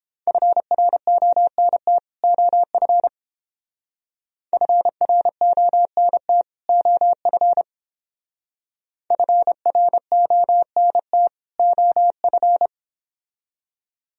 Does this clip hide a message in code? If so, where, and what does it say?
Yes: Morse code from 0.3 to 12.7 s, heard as a keyed tone, about 30 wpm, 700 Hz. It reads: FRONT OF FRONT OF FRONT OF